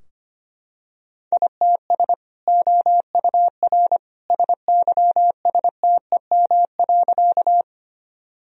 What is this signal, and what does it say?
Morse code, heard as a keyed tone, about 25 wpm, 700 Hz. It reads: ITS OUR SYSTEM.
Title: Sentences from Top 1000 Words - Review Part 3 - 25wpm